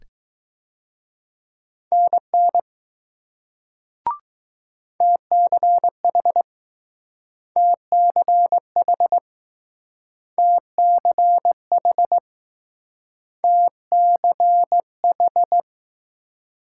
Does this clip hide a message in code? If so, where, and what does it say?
Yes: Morse code from 1.9 to 15.6 s, heard as a keyed tone, about 20 wpm, 700 Hz. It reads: NN TCH TCH TCH TCH